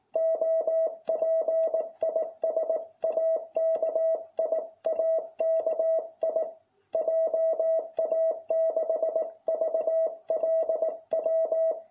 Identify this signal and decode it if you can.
Morse code, heard as a keyed tone, about 18 wpm, 660 Hz. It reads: OP IS AKI AKI JA6VLW